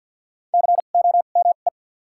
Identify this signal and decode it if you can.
Morse code, heard as a keyed tone, about 53 wpm, 700 Hz. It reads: C O M E
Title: 2x Character Spacing - Top 200 Words - 50wpm